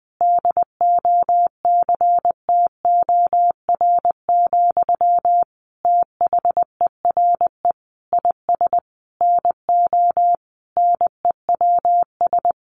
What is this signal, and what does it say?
Morse code, heard as a keyed tone, about 19 wpm, 700 Hz. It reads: DOCTOR, THERE IS NO NEWS
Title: Sentences from Top 1500 Words 20wpm